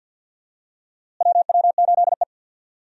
Morse code, heard as a keyed tone, about 47 wpm, 700 Hz. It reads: WW8E
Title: USA Call Signs - 4 Characters - Repeated 2x - 50wpm